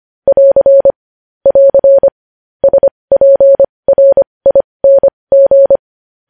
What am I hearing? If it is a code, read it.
Morse code, heard as a keyed tone, about 25 wpm, spaced as for 20 wpm, 550 Hz. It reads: <AR> <AR> SPRING
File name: podcast / News headlines-1x-25x20